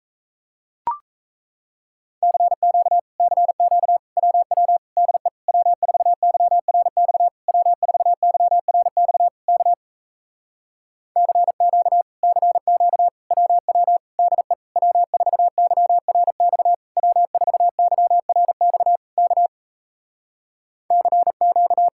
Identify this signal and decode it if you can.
Morse code, heard as a keyed tone, about 38 wpm, 700 Hz. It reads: CQ CQ WW DE W4YRX W4YRX K CQ CQ WW DE W4YRX W4YRX K CQ